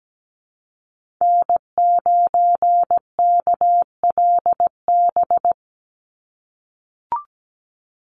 Morse code, heard as a keyed tone, about 17 wpm, 700 Hz. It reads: N9KLB
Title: USA Call Signs 17wpm